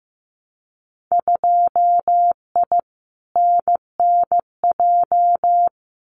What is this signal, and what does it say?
Morse code, heard as a keyed tone, about 15 wpm, 700 Hz. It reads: 2I NNJ
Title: Winter Field Day Exchanges Encore 15wpm